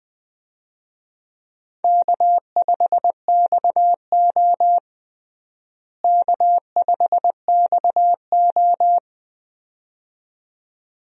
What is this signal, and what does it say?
Morse code, heard as a keyed tone, about 21 wpm, 700 Hz. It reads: K5XO K5XO
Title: USA Call Signs - 4 Characters - Repeated 2x - 20wpm